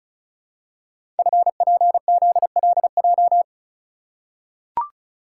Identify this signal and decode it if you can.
Morse code, heard as a keyed tone, about 36 wpm, 700 Hz. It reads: FPZLJ
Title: Code Groups - 5 Letter:Numbers - 35wpm